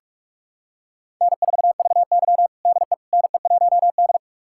Morse code, heard as a keyed tone, about 48 wpm, 700 Hz. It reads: N4VY DE NE1D
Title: USA Callsign DE Callsign - 4 Characters - 45wpm